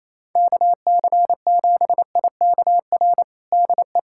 Morse code, heard as a keyed tone, about 28 wpm, 700 Hz. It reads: KC7IKR DE
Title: USA Callsign DE Callsign 28wpm